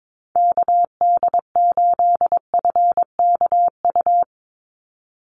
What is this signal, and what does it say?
Morse code, heard as a keyed tone, about 22 wpm, 700 Hz. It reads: KD8FKU
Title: USA Call Signs 22wpm